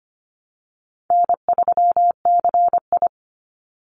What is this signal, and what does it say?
Morse code, heard as a keyed tone, about 25 wpm, 700 Hz. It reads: N3CI
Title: Call Signs Repeated 2x - 25wpm